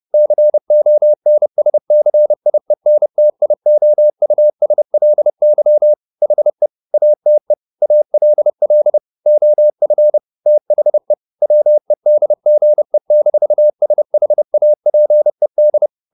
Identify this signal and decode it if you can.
Morse code, heard as a keyed tone, about 30 wpm, 600 Hz. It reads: CONSCIENTIOUSLY HE ATE ALL OF THE WEDGE-SHAPED
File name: the old man and the sea / the old man and the sea 1 30wpm